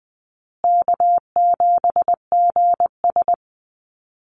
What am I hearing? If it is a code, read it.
Morse code, heard as a keyed tone, about 20 wpm, 700 Hz. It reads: K7GS
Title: Call Signs Repeated 3x - 20wpm